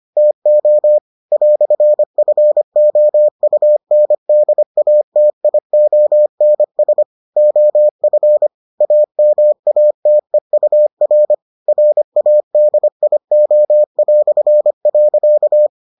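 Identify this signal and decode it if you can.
Morse code, heard as a keyed tone, about 25 wpm, 600 Hz. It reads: TO "FOUNDATIONS OF AMATEUR RADIO".
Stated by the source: Morse edition of the Foundations of Amateur Radio episodes.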